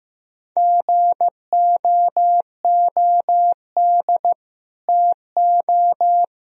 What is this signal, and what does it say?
Morse code, heard as a keyed tone, about 15 wpm, 700 Hz. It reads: GOOD TO
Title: Speed-Racing - Sets of 4 Words 15wpm